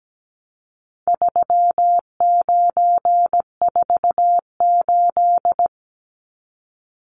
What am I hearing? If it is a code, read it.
Morse code, heard as a keyed tone, about 17 wpm, 700 Hz. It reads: 3948
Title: Numbers - 4-Digit 17wpm